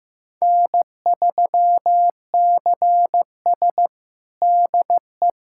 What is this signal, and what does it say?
Morse code, heard as a keyed tone, about 15 wpm, 700 Hz. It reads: N3CS DE
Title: USA Callsign DE Callsign - 4 Characters - 15wpm